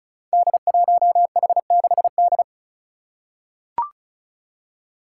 Morse code, heard as a keyed tone, about 36 wpm, 700 Hz. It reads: D1H6D
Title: Five Challenging Letter-Number Combinations 35wpm